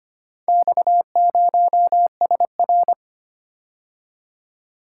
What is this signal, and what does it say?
Morse code, heard as a keyed tone, about 25 wpm, 700 Hz. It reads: X0SR
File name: practice / 25wpm / Four Letter-Number Combinations - 25wpm